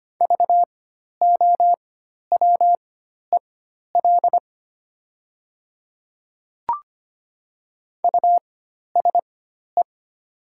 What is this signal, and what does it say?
Morse code, heard as a keyed tone, about 25 wpm, spaced as for 12 wpm, 700 Hz. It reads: VOWEL USE
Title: 4x Character Spacing - Top 500 Words - 25wpm